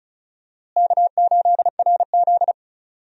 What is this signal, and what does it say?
Morse code, heard as a keyed tone, about 36 wpm, 700 Hz. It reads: K8RZ
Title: USA Call Signs - 4 Characters - Repeated 2x - 35wpm